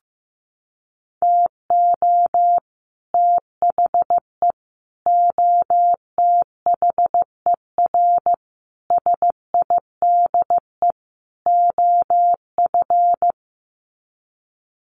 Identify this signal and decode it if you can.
Morse code, heard as a keyed tone, about 15 wpm, 700 Hz. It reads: TO THE OTHER SIDE OF
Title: Sets of 5 Words Encore 15wpm